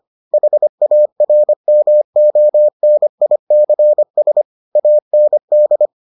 Morse code, heard as a keyed tone, about 25 wpm, 600 Hz. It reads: HARMONICS AND